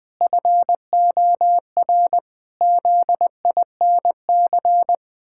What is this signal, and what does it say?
Morse code, heard as a keyed tone, about 20 wpm, 700 Hz. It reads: FOR ZINC
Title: Pangram Challenge v2 - 20wpm